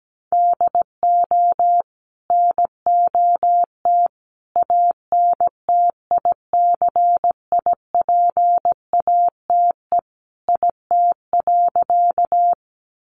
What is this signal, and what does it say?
Morse code, heard as a keyed tone, about 17 wpm, 700 Hz. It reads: DO NOT ANTICIPATE IT.
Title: Sentences from Top 1500 Words 17wpm